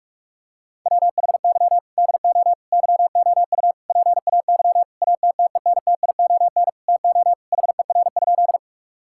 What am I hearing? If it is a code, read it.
Morse code, heard as a keyed tone, about 46 wpm, 700 Hz. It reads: WHY DO YOU PAY ATTENTION TO HER?
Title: Sentences from Top 1000 Words - Review Part 6 - 45wpm